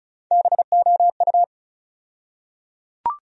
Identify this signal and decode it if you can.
Morse code, heard as a keyed tone, about 34 wpm, 700 Hz. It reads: BOU
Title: USA Call Sign Postfixes - 3 Letters 35wpm